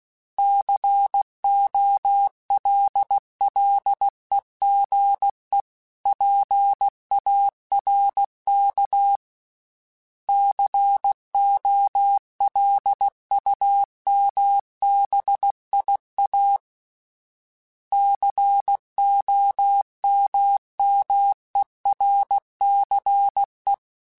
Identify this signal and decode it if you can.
Morse code, heard as a keyed tone, about 16 wpm, 800 Hz. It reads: COLLEGE PARK COLUMBIA COMMERCE
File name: fromSMs / FromSMs mp3s / Cities@15